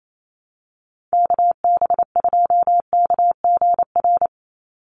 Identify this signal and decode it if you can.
Morse code, heard as a keyed tone, about 28 wpm, 700 Hz. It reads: KB2KGR
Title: USA Call Signs 28wpm